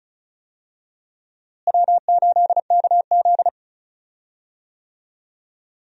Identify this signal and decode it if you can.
Morse code, heard as a keyed tone, about 36 wpm, 700 Hz. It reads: W8KZ
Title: USA Call Signs - 4 Characters - Repeated 2x - 35wpm